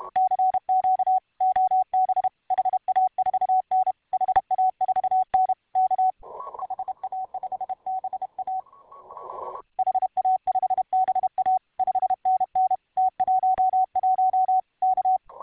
Morse code, heard as a keyed tone, about 32 wpm, 750 Hz. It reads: CQ OB HA4N HA4N K HA5BA HA5BA 5NN T11 K